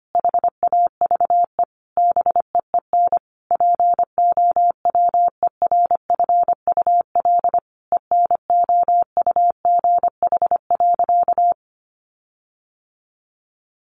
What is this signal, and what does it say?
Morse code, heard as a keyed tone, about 25 wpm, 700 Hz. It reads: HAVE BEEN POWERFUL ENOUGH.
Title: Sentences from Top 1300 Words 25wpm